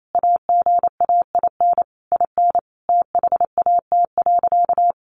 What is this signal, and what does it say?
Morse code, heard as a keyed tone, about 28 wpm, 700 Hz. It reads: AGAIN IN THAT.
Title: Sentences from Top 200 Words 28wpm